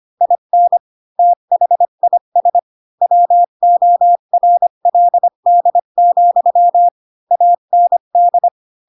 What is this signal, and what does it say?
Morse code, heard as a keyed tone, about 25 wpm, 700 Hz. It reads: IN THIS WORLD, AND
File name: Pinocchio / Pinocchio ch4 20 25 ik1hge